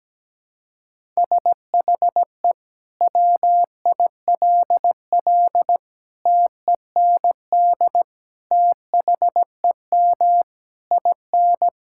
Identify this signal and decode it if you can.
Morse code, heard as a keyed tone, about 17 wpm, 700 Hz. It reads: SHE WILL TEND THEM IN
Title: Sentences from Top 1000 Words - Review Part 5 - 17wpm